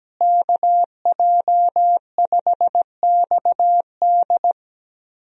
Morse code, heard as a keyed tone, about 17 wpm, 700 Hz. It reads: KJ5XD
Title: USA Call Signs - 5 Characters - Repeated 2x - 17wpm